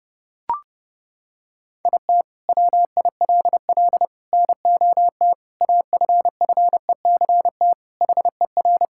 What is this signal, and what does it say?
Morse code, heard as a keyed tone, about 32 wpm, 700 Hz. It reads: IT WILL NOT AFFECT HER
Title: Sentences from Top 1000 Words - Review Part 6 - 30wpm